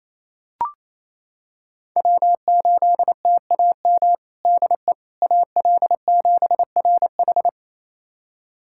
Morse code, heard as a keyed tone, about 28 wpm, 700 Hz. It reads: W8TAM DE AL7RH
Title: USA Callsign DE Callsign - 5 Characters - 28wpm